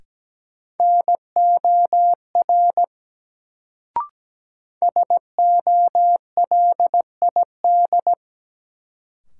Morse code, heard as a keyed tone, about 17 wpm, 700 Hz. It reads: NOR SOLID
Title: Sentences from Top 1500 Words - New Word Drill - 17wpm